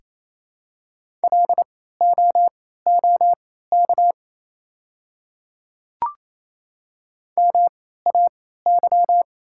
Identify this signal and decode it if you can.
Morse code, heard as a keyed tone, about 28 wpm, spaced as for 16 wpm, 700 Hz. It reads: LOOK MAY
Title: 3x Character Spacing - Top 500 Words - 28wpm